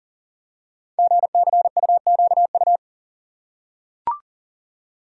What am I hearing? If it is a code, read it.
Morse code, heard as a keyed tone, about 43 wpm, 700 Hz. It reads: GCUQU
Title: Code Groups - 5 Letters - 40wpm